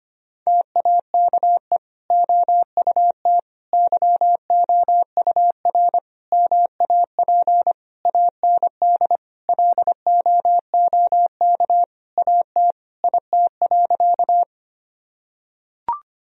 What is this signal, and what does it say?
Morse code, heard as a keyed tone, about 25 wpm, 700 Hz. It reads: TAKE OUT YOUR MAP AND LOOK AT IT.
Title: Sentences from Top 1000 Words - Review Part 2 - 25wpm